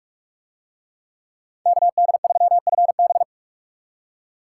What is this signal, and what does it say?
Morse code, heard as a keyed tone, about 41 wpm, 700 Hz. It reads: KD3FB